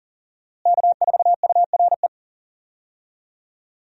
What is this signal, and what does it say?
Morse code, heard as a keyed tone, about 40 wpm, 700 Hz. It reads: K4URE
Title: USA Call Signs - 5 Characters - Repeated 3x - 40wpm